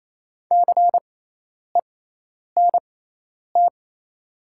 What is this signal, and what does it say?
Morse code, heard as a keyed tone, about 28 wpm, spaced as for 10 wpm, 700 Hz. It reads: CENT
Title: 6x Character Spacing - Top 500 Words - 28wpm